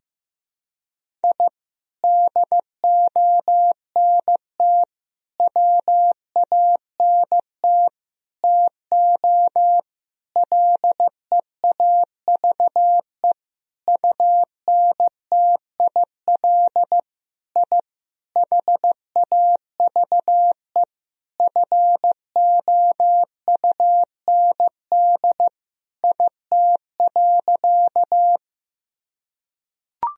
Morse code, heard as a keyed tone, about 15 wpm, 700 Hz. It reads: I DONT WANT TO LEAVE UNTIL I HAVE FOUND IT.
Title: Sentences from Top 1000 Words - Review Part 1 - 15wpm